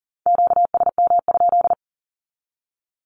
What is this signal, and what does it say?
Morse code, heard as a keyed tone, about 40 wpm, 700 Hz. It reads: QSM?
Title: Q-Codes 40wpm